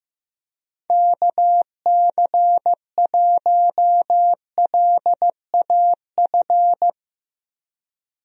Morse code, heard as a keyed tone, about 15 wpm, 700 Hz. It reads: KC1LAF
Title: USA Call Signs - 6 Characters - Repeated 3x - 15wpm